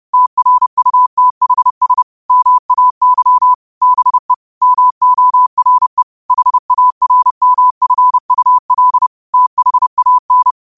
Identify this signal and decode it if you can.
Morse code, heard as a keyed tone, about 30 wpm, 1000 Hz. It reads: TRUTHS MAY BE MORE HARMFUL THAN